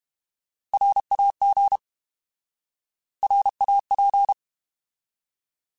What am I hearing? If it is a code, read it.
Morse code, heard as a keyed tone, about 32 wpm, 800 Hz. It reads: RAG RAP
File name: Alphabetical / Alphabetical mp3s / Rs @30